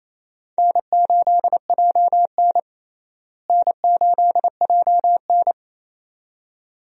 Morse code, heard as a keyed tone, about 28 wpm, 700 Hz. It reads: N8JN N8JN
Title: USA Call Signs - 4 Characters - Repeated 2x - 28wpm